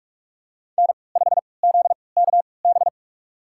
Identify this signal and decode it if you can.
Morse code, heard as a keyed tone, about 42 wpm, spaced as for 25 wpm, 700 Hz. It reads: N5ZKB
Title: USA Call Signs - 5 Characters - 3x Character Spacing - 45wpm